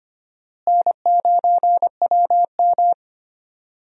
Morse code, heard as a keyed tone, about 25 wpm, 700 Hz. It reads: N9WM
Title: USA Call Signs - 4 Characters - Repeated 2x - 25wpm